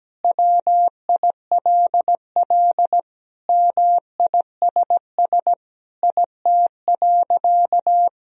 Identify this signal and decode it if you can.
Morse code, heard as a keyed tone, about 17 wpm, 700 Hz. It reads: WILL MISS IT.
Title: Sentences from Top 1000 Words - Review Part 3 - 17wpm